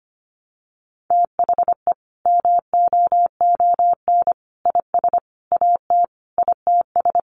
Morse code, heard as a keyed tone, about 25 wpm, 700 Hz. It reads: THE MOON IS AT ITS
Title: Sentences from Top 1300 Words 25wpm